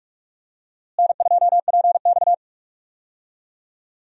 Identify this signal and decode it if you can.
Morse code, heard as a keyed tone, about 46 wpm, 700 Hz. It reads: N2PX
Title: USA Call Signs - 4 Characters - Repeated 3x - 45wpm